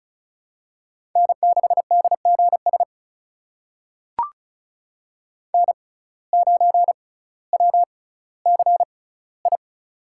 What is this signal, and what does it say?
Morse code, heard as a keyed tone, about 35 wpm, spaced as for 18 wpm, 700 Hz. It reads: N6DGS N9WCI
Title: USA Call Signs - 5 Characters - 6x Character Spacing - 35wpm